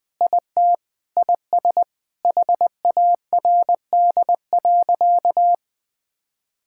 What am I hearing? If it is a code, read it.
Morse code, heard as a keyed tone, about 20 wpm, 700 Hz. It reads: IT IS HARD.
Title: Sentences with 4-Word Phrases - 20wpm